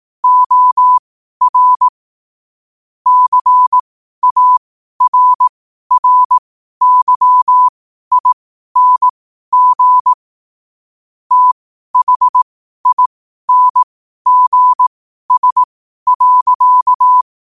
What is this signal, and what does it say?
Morse code, heard as a keyed tone, about 18 wpm, spaced as for 12 wpm, 1000 Hz. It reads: OR CARRYING THINGS.
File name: qotdfiles / 12wpm